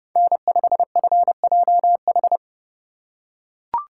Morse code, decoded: N5FJH

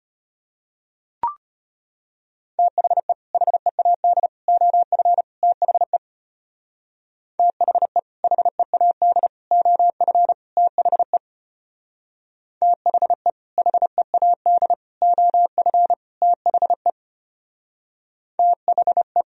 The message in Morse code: THE HEAD OF THE THE HEAD OF THE THE HEAD OF THE THE

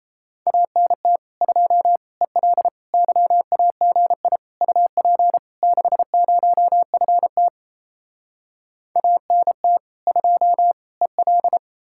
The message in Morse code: ANT 2 EL YAGI UP 60FT ANT 2 EL

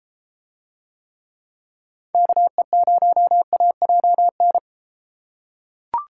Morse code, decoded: KE0AJN

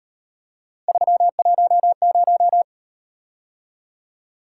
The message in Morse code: 310

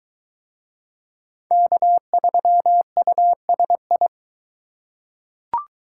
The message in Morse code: K3USI

